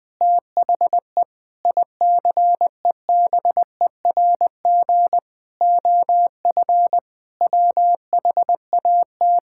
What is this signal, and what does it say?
Morse code, heard as a keyed tone, about 20 wpm, 700 Hz. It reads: THE ICEBERG OF WHAT
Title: Sentences with 8-Word Phrases - 20wpm